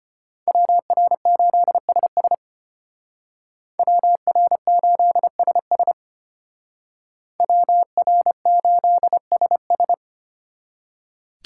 Morse code, heard as a keyed tone, about 33 wpm, 700 Hz. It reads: WR8SS WR8SS WR8SS